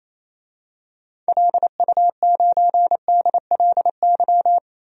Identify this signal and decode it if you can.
Morse code, heard as a keyed tone, about 28 wpm, 700 Hz. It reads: LU9DLY
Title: DXCC-100 Call Signs - 6 Characters - 28wpm